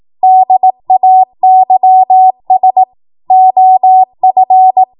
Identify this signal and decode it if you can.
Morse code, heard as a keyed tone, about 18 wpm, 750 Hz. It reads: DAYS OF
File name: Morse / 18WPM